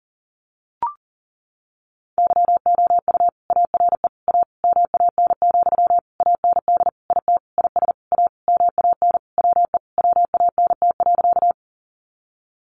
Morse code, heard as a keyed tone, about 37 wpm, 700 Hz. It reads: YOU ARE A MAN, AND IT IS A MAN WE WANT.
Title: Sentences from Top 200 Words 40wpm